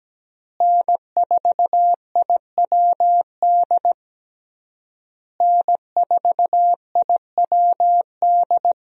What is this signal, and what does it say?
Morse code, heard as a keyed tone, about 17 wpm, 700 Hz. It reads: N4IWD N4IWD
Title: USA Call Signs - 5 Characters - Repeated 3x - 17wpm